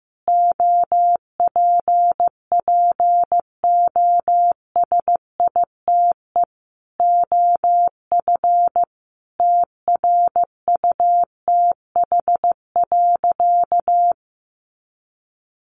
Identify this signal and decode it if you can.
Morse code, heard as a keyed tone, about 15 wpm, 700 Hz. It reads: OPPOSITE OF TRUTH.
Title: Sentences from Top 1300 Words 15wpm